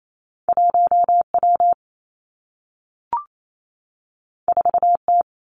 Morse code, heard as a keyed tone, about 28 wpm, 700 Hz. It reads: 1 W 4 T
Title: Two Challenging Letter-Number Combinations 28wpm